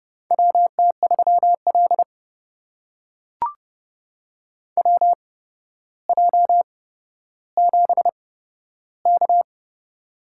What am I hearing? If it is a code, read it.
Morse code, heard as a keyed tone, about 30 wpm, spaced as for 13 wpm, 700 Hz. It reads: WT3L WJ7K
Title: USA Call Signs - 4 Characters - 8x Character Spacing - 30wpm